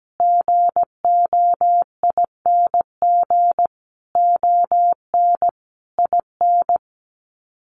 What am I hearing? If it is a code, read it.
Morse code, heard as a keyed tone, about 17 wpm, 700 Hz. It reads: GOING ON IN
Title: Sets of 5 Words Encore 17wpm